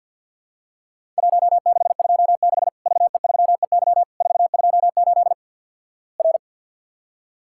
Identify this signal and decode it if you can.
Morse code, heard as a keyed tone, about 53 wpm, 700 Hz. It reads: 1626 VE3EY 428 R
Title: Worked All Europe DX Contest - QTC Exchange with Repeats - 50wpm